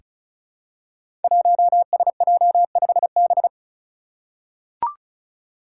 Morse code, heard as a keyed tone, about 36 wpm, 700 Hz. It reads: 1SJ5B
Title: Five Challenging Letter-Number Combinations 35wpm